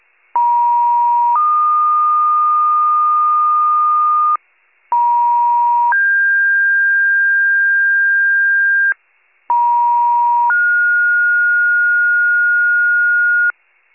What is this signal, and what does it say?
Two-tone page, heard as a one-second tone, then a three-second tone, about 950 then 1220 Hz, about 950 then 1600 Hz, about 950 then 1430 Hz, with noise. It is heard from 0.3 to 13.5 s.